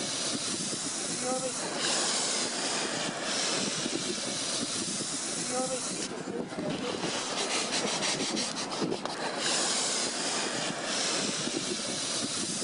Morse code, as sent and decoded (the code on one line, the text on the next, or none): none